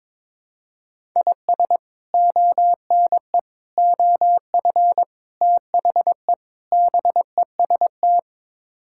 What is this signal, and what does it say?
Morse code, heard as a keyed tone, about 22 wpm, 700 Hz. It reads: IS ONE OF THE BEST
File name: practice / 22wpm / Build A Phrase - 5-Word Phrases - 22wpm